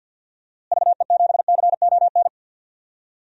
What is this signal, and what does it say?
Morse code, heard as a keyed tone, about 47 wpm, 700 Hz. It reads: VE7CON
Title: DXCC-100 Call Signs - 6 Characters - Repeated 2x - 50wpm